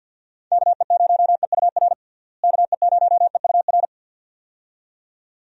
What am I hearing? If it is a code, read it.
Morse code, heard as a keyed tone, about 51 wpm, 700 Hz. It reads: KE0EUR KE0EUR
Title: USA Call Signs - 6 Characters - Repeated 2x - 50wpm